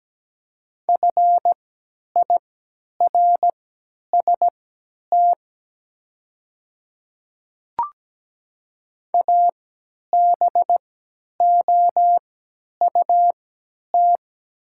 Morse code, heard as a keyed tone, about 17 wpm, spaced as for 10 wpm, 700 Hz. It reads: FIRST ABOUT
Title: 3x Character Spacing - Top 100 Words - 17wpm